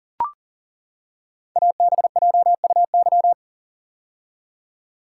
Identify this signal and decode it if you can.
Morse code, heard as a keyed tone, about 39 wpm, 690 Hz. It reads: ABJUY
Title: Code Groups - 5 Letters - 40wpm